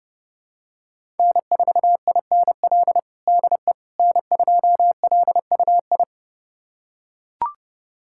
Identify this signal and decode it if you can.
Morse code, heard as a keyed tone, about 32 wpm, 700 Hz. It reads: N4INL DE N2LUI
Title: USA Callsign DE Callsign - 5 Characters - 30wpm